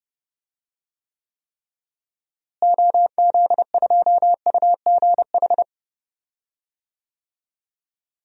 Morse code, heard as a keyed tone, about 31 wpm, 700 Hz. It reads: OZ2UGH
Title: DXCC-100 Call Signs - 6 Characters - Repeated 3x - 30wpm